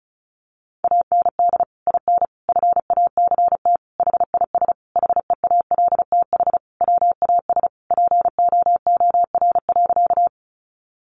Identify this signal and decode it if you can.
Morse code, heard as a keyed tone, about 34 wpm, 700 Hz. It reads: AND IN FACT HIS HEALTH WAS POOR.
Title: Sentences from Top 1000 Words 35wpm